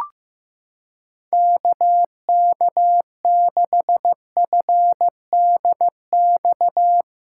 Morse code, heard as a keyed tone, about 15 wpm, 700 Hz. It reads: KK6FDX
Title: USA Callsign DE Callsign - 6 Characters - 15wpm